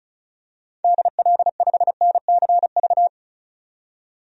Morse code, heard as a keyed tone, about 36 wpm, 700 Hz. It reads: DL5NCV
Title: DXCC-100 Call Signs - 6 Characters - 35wpm